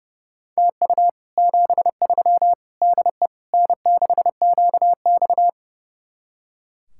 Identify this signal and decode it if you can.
Morse code, heard as a keyed tone, about 36 wpm, 700 Hz. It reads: TU 73 DE N6QX
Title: Speed-Racing Common QSO Phrases - 30wpm